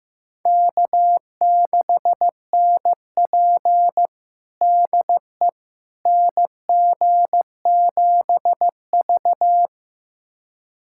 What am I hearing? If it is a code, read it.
Morse code, heard as a keyed tone, about 15 wpm, 700 Hz. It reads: K6NP DE NG7V
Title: USA Callsign DE Callsign - 4 Characters - 15wpm